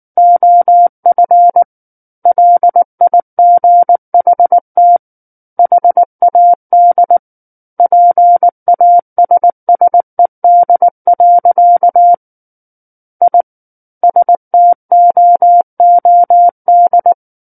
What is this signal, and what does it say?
Morse code, heard as a keyed tone, about 19 wpm, 700 Hz. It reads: OF LIGHT HAD PASSED. I STOOD
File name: Book One / 10-Chapter 10 - 19 WPM